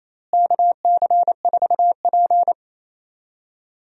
Morse code, heard as a keyed tone, about 28 wpm, 700 Hz. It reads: KC4P
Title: USA Call Signs - 4 Characters - Repeated 2x - 28wpm